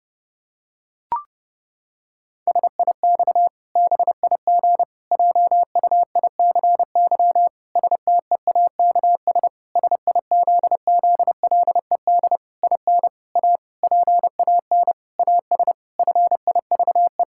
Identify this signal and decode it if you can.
Morse code, heard as a keyed tone, about 30 wpm, 700 Hz. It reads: SIX BIG JUICY STEAKS SIZZLED IN A PAN AS FIVE
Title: Pangram Challenge v2 - 30wpm